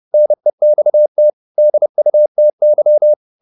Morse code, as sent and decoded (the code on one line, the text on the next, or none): -. -..- - / -.. ..- - -.--
NEXT DUTY